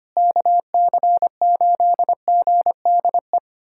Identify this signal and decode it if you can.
Morse code, heard as a keyed tone, about 25 wpm, 700 Hz. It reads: KC8GDE
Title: USA Callsign DE Callsign - 6 Characters - 25wpm